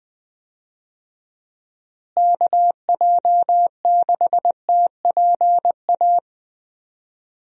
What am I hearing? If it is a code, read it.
Morse code, heard as a keyed tone, about 20 wpm, 700 Hz. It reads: KJ6TPA
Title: USA Call Signs - 6 Characters - Repeated 2x - 20wpm